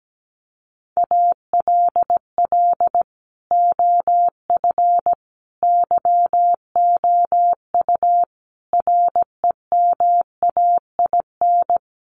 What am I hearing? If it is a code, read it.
Morse code, heard as a keyed tone, about 17 wpm, 700 Hz. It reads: ALL OF YOU REMAIN
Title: Sentences from Top 1200 Words 17wpm